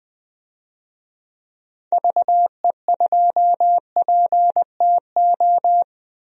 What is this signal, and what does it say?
Morse code, heard as a keyed tone, about 20 wpm, 700 Hz. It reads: VE2PTO
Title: DXCC-100 Call Signs - 6 Characters - Repeated 3x - 20wpm